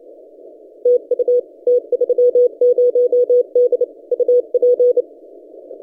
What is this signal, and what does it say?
Morse code, heard as a keyed tone, about 30 wpm, 490 Hz. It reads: TU T30D UP
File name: t30d-12cw